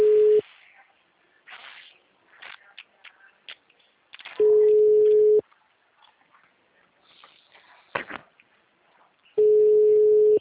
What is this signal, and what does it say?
Ringback tone (Europe, 425 Hz).
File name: out- -2022- - - .31722